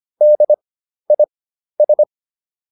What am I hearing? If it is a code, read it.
Morse code, heard as a keyed tone, about 25 wpm, spaced as for 12 wpm, 600 Hz. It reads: DIS